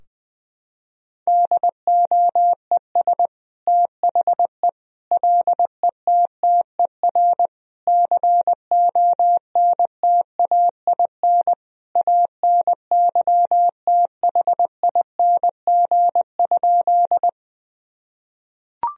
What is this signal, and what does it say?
Morse code, heard as a keyed tone, about 20 wpm, 700 Hz. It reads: DOES THE LETTER CONTAIN ANYTHING?
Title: Sentences from Top 1000 Words - Review Part 2 - 20wpm